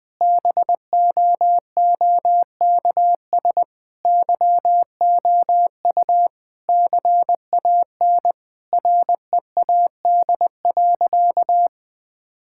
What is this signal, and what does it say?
Morse code, heard as a keyed tone, about 19 wpm, 700 Hz. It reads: BOOKS YOU CAN READ.
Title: Sentences with 8-Word Phrases - 20wpm